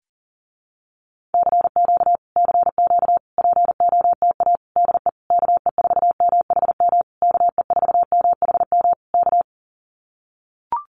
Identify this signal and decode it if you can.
Morse code, heard as a keyed tone, about 38 wpm, 700 Hz. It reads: CQ CQ POTA DE KE4MHM KE4MHM K
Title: Common QSO Phrases Encore 40wpm